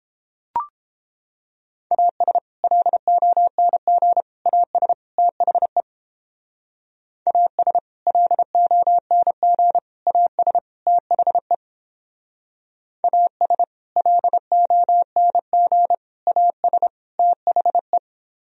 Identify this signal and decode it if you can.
Morse code, heard as a keyed tone, about 29 wpm, 700 Hz. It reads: AS LONG AS THE AS LONG AS THE AS LONG AS THE